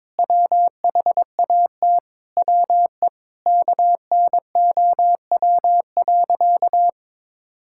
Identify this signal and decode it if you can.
Morse code, heard as a keyed tone, about 22 wpm, 700 Hz. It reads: WHAT WE KNOW.
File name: practice / 22wpm / Sentences with 8-Word Phrases - 22wpm